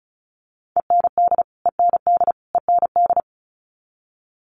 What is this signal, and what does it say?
Morse code, heard as a keyed tone, about 34 wpm, 700 Hz. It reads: END END END
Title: Top 300 Words Repeated 35wpm